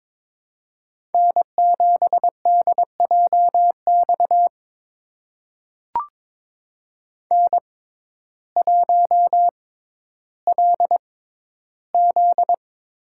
Morse code, heard as a keyed tone, about 23 wpm, spaced as for 12 wpm, 700 Hz. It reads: N7DJX N1LZ
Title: USA Call Signs - 5 Characters - 6x Character Spacing - 22wpm